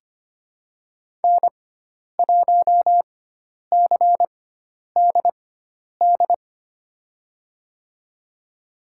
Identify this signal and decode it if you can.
Morse code, heard as a keyed tone, about 25 wpm, spaced as for 10 wpm, 700 Hz. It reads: N1CDD
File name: Lesson 16g - Letter c - Call Signs 25wpm